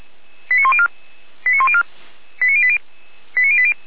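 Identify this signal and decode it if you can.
Five-tone selective call (ZVEI), heard as quick runs of five tones, tones of about 70 ms. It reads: 89104 89104 89080 89080
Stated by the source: ZVEI five-tone selective call received by SDR.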